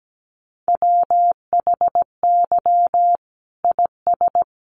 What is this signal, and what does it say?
Morse code, heard as a keyed tone, about 17 wpm, 700 Hz. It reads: WHY IS
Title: Sentences from Top 1200 Words 17wpm